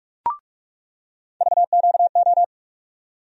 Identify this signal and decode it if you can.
Morse code, heard as a keyed tone, about 46 wpm, 700 Hz. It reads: VQO
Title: USA Call Sign Postfixes - 3 Letters 45wpm